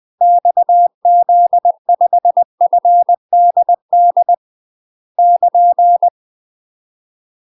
Morse code, heard as a keyed tone, about 20 wpm, 700 Hz. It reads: XZ5FDD <KN>
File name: UK MASTER QSOs TRAINING / 20WPM / UK MASTER QSOs 0012